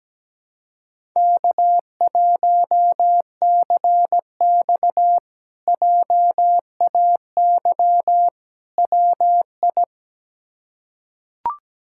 Morse code, heard as a keyed tone, about 17 wpm, 700 Hz. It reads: K1CX JAY WI
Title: K1USN SST QSO Practice - 17wpm